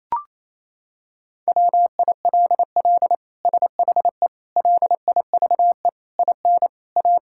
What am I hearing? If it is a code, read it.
Morse code, heard as a keyed tone, about 27 wpm, 700 Hz. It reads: WILL SHE LIVE IN A
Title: Sentences from Top 1000 Words - Review Part 6 - 28wpm